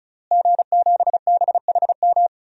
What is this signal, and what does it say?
Morse code, heard as a keyed tone, about 36 wpm, 700 Hz. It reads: G7BHM
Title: DXCC-100 Call Signs - 5 Characters - Repeated 3x - 35wpm